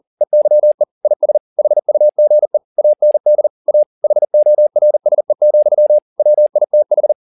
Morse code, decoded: EYE IS HUGE AND A HORSE, WITH

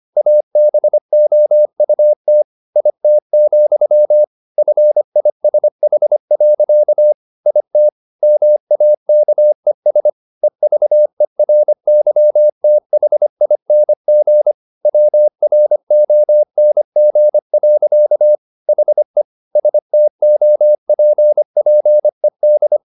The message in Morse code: ABOUT IT, FISH. IT MAKES EVERYTHING WRONG. HE STOPPED

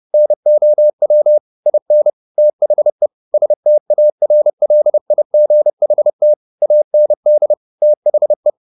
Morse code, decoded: NOW IN THE STARLIGHT AND THE

